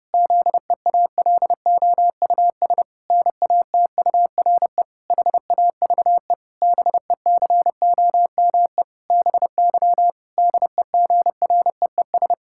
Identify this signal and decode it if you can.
Morse code, heard as a keyed tone, about 30 wpm, 700 Hz. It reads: ZEALOUS NATURE HAVE BECOME BY DEGREES
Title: Pangram Challenge v2 - 30wpm